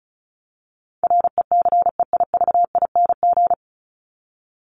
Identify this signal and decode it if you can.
Morse code, heard as a keyed tone, about 34 wpm, 700 Hz. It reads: RECEIVING
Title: ING Suffix - Top 100 Words with Base Words 35wpm